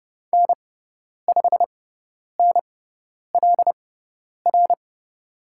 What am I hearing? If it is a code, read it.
Morse code, heard as a keyed tone, about 30 wpm, spaced as for 10 wpm, 700 Hz. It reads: N5NLR
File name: Lesson 16g - Letter c - Call Signs 30wpm